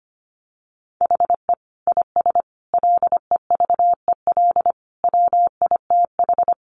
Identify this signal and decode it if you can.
Morse code, heard as a keyed tone, about 25 wpm, 700 Hz. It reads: HE IS LEVEL WITH
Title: Sentences from Top 700 Words 25wpm